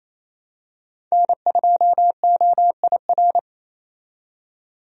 Morse code, heard as a keyed tone, about 28 wpm, 700 Hz. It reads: N2OIR
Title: USA Call Signs - 5 Characters - 28wpm